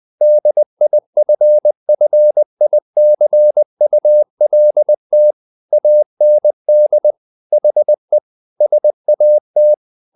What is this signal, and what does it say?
Morse code, heard as a keyed tone, about 20 wpm, 600 Hz. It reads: DIFFICULT AND HE SAT
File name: the old man and the sea part 2 20wpm